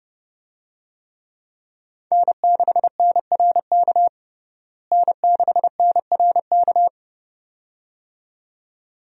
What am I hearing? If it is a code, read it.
Morse code, heard as a keyed tone, about 30 wpm, 700 Hz. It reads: N6NRK N6NRK
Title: USA Call Signs - 5 Characters - Repeated 2x - 30wpm